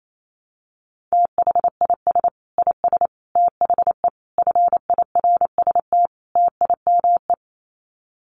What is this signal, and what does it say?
Morse code, heard as a keyed tone, about 28 wpm, 700 Hz. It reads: THIS IS THE FIRST TIME
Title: Sets of 5 Words 28wpm